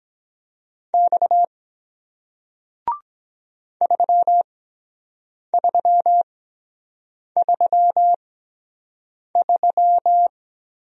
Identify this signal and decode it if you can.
Morse code, heard as a keyed tone, about 23 wpm, spaced as for 7 wpm, 700 Hz. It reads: X 3333